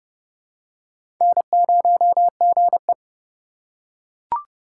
Morse code, heard as a keyed tone, about 30 wpm, 700 Hz. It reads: N0GE
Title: USA Call Signs - 4 Characters - 6x Character Spacing - 30wpm